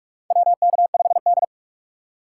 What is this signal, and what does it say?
Morse code, heard as a keyed tone, about 46 wpm, 700 Hz. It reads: WK5D
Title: USA Call Signs - 4 Characters - 6x Character Spacing - 45wpm